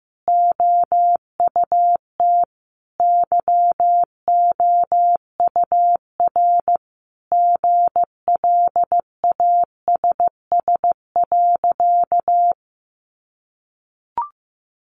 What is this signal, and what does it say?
Morse code, heard as a keyed tone, about 15 wpm, 700 Hz. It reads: OUT YOUR GLASS.
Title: Sentences from Top 1300 Words 15wpm